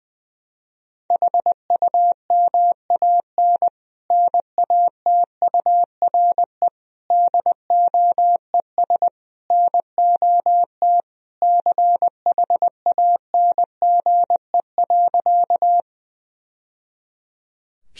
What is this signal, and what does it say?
Morse code, heard as a keyed tone, about 20 wpm, 700 Hz. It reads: HUMAN NATURE DOES NOT CHANGE.
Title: Sentences from Top 1000 Words - Review Part 3 - 20wpm